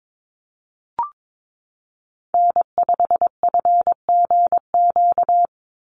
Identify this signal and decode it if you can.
Morse code, heard as a keyed tone, about 22 wpm, 700 Hz. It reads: N5FGQ
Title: USA Call Signs 22wpm